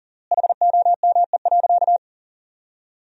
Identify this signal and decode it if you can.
Morse code, heard as a keyed tone, about 40 wpm, 700 Hz. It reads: HOME.
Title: 3x Word-Spacing - Sentences from Top 400 Words - 40wpm